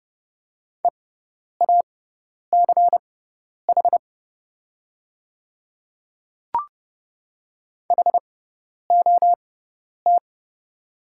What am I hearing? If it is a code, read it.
Morse code, heard as a keyed tone, about 31 wpm, spaced as for 10 wpm, 700 Hz. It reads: EACH HOT
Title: 6x Character Spacing - Top 100 Words - 30wpm